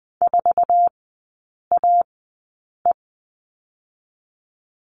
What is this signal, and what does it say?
Morse code, heard as a keyed tone, about 21 wpm, spaced as for 8 wpm, 700 Hz. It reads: 4AE
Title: 047 Lesson 9e - Number 4 - Three Characters 20wpm